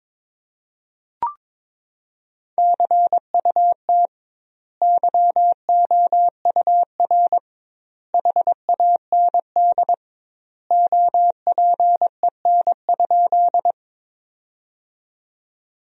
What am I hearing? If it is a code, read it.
Morse code, heard as a keyed tone, about 22 wpm, 700 Hz. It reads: CUT YOUR HAND OPEN?
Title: 2x Word-Spacing - Sentences from Top 300 Words - 22wpm